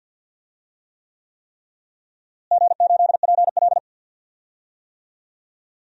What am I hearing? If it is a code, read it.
Morse code, heard as a keyed tone, about 50 wpm, 700 Hz. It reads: G8PL
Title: DXCC-100 Call Signs - 4 Characters - Repeated 3x - 50wpm